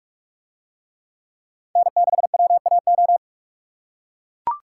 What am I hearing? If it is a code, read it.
Morse code, heard as a keyed tone, about 41 wpm, 700 Hz. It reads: N6WAO